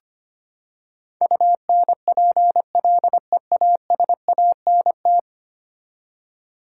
Morse code, heard as a keyed tone, about 25 wpm, 700 Hz. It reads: UNPLEASANT
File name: Ten Letter Words - 25wpm